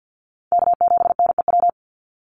Morse code, heard as a keyed tone, about 48 wpm, 700 Hz. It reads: K7NEW
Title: NA SOTA Chaser Callsigns 50wpm